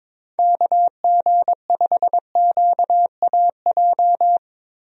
Morse code, heard as a keyed tone, about 22 wpm, 700 Hz. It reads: KG5QAJ
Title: USA Call Signs - 6 Characters - Repeated 2x - 22wpm